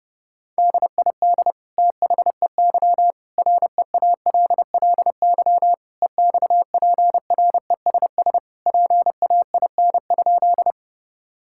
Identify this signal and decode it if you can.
Morse code, heard as a keyed tone, about 32 wpm, 700 Hz. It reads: DID THEY REALLY EXPRESS PAIN?
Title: Sentences from Top 1600 Words 30wpm